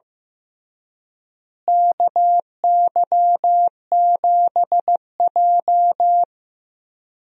Morse code, heard as a keyed tone, about 15 wpm, 700 Hz. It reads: KY7J
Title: USA Call Signs - 4 Characters - Repeated 3x - 15wpm